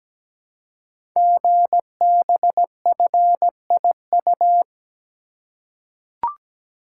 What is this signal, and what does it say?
Morse code, heard as a keyed tone, about 17 wpm, 700 Hz. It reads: GBFIU
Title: Code Groups - 5 Letters - 17wpm